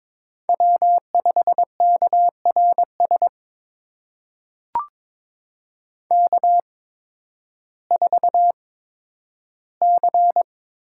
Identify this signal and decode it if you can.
Morse code, heard as a keyed tone, about 22 wpm, spaced as for 12 wpm, 700 Hz. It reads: W5KRS K4C
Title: USA Call Signs - 5 Characters - 8x Character Spacing - 22wpm